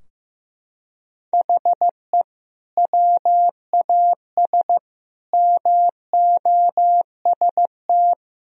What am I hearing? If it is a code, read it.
Morse code, heard as a keyed tone, about 15 wpm, 700 Hz. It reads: HE WAS MOST
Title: Sentences from Top 1000 Words - Review Part 4 - 15wpm